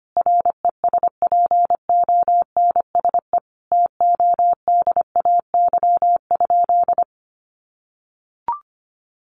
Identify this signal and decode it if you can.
Morse code, heard as a keyed tone, about 25 wpm, 700 Hz. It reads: RESPONSE TODAY?
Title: Sentences from Top 1400 Words 25wpm